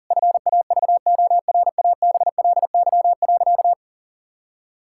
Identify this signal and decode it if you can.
Morse code, heard as a keyed tone, about 37 wpm, 700 Hz. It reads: FAVORABLY.
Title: Pangram Challenge v2 - 40wpm